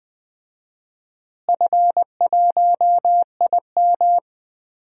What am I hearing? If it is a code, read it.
Morse code, heard as a keyed tone, about 20 wpm, 700 Hz. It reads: F1IM
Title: DXCC-100 Call Signs - 4 Characters - Repeated 3x - 20wpm